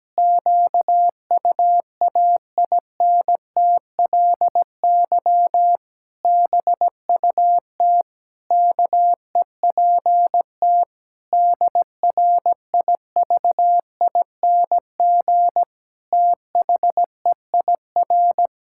Morse code, heard as a keyed tone, about 17 wpm, 700 Hz. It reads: QUAINTLY BUT KEPT DRIVING THEIR
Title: Pangram Challenge v2 - 17wpm